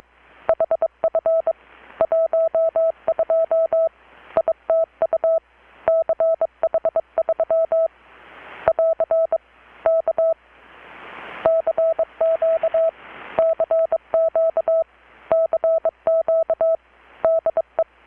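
Morse code, heard as a keyed tone, about 22 wpm, 650 Hz. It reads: HF 12 ITU CH3 <AR> K CQ CQ CQ DE
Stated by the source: Morse from KPH's Night of Nights XX maritime radiotelegraphy event, received in Chiapas, Mexico about 3,800 km away on a Kenwood R-600 with a grounded 10 m long-wire antenna.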